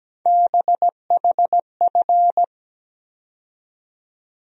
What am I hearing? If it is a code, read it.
Morse code, heard as a keyed tone, about 17 wpm, 700 Hz. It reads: BHF
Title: USA Call Sign Postfixes - 3 Letters 17wpm